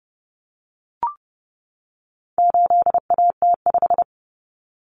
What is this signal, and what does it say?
Morse code, heard as a keyed tone, about 30 wpm, 700 Hz. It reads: 8AT5
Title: Common Cut Numbers - 4-Digit 30wpm